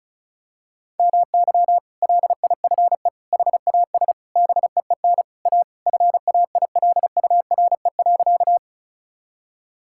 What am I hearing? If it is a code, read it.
Morse code, heard as a keyed tone, about 34 wpm, 700 Hz. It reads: MY LIFE HAS BEEN A FAILURE.
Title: Sentences from Top 1000 Words - Review Part 6 - 35wpm